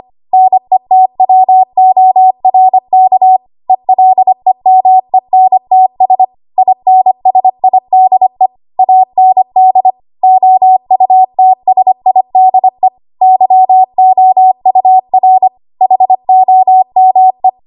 Morse code, heard as a keyed tone, about 25 wpm, 750 Hz. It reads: NETWORK ELEMENTS INSIDE AND OUTSIDE YOUR HOME
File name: Archive / 25 WPM / 25WPM